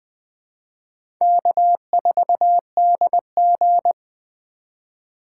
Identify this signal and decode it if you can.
Morse code, heard as a keyed tone, about 20 wpm, 700 Hz. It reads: K4DG
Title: USA Call Signs - 4 Characters - 4x Character Spacing - 20wpm